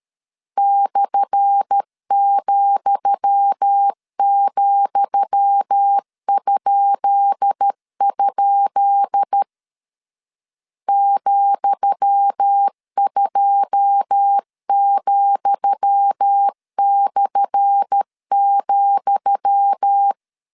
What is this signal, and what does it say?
Morse code, heard as a keyed tone, about 13 wpm, 800 Hz. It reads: /,,?? ,2,/,
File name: LEZIONI / 10 SIGNS / 10D COMA 12wpm 2544